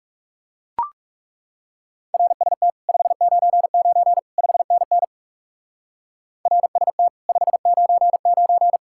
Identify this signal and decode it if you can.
Morse code, heard as a keyed tone, about 39 wpm, 700 Hz. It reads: RST 599 5NN RST 599